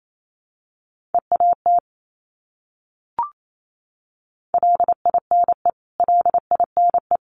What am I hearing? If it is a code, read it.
Morse code, heard as a keyed tone, about 28 wpm, 700 Hz. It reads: EAT LINE LINE
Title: Top 500 Words Repeated 28wpm